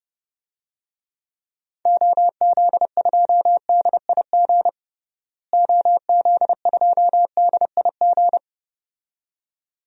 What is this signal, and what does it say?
Morse code, heard as a keyed tone, about 32 wpm, 700 Hz. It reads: OZ2DIG OZ2DIG
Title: DXCC-100 Call Signs - 6 Characters - Repeated 2x - 30wpm